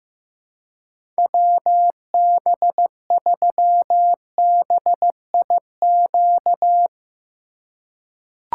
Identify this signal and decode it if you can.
Morse code, heard as a keyed tone, about 15 wpm, 700 Hz. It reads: WB3BIQ
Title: K1USN SST QSO Practice - 15wpm